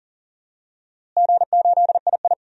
Morse code, heard as a keyed tone, about 37 wpm, 700 Hz. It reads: G8II
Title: DXCC-100 Call Signs - 4 Characters - Repeated 2x - 40wpm